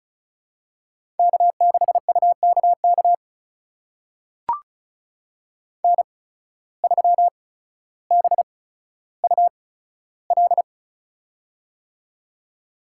Morse code, heard as a keyed tone, about 36 wpm, spaced as for 15 wpm, 700 Hz. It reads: K6UKK N3BUL
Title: USA Call Signs - 5 Characters - 8x Character Spacing - 35wpm